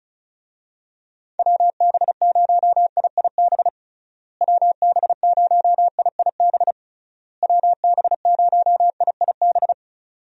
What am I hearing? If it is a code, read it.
Morse code, heard as a keyed tone, about 36 wpm, 700 Hz. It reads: WB0IIB WB0IIB WB0IIB
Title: USA Call Signs - 6 Characters - Repeated 3x - 35wpm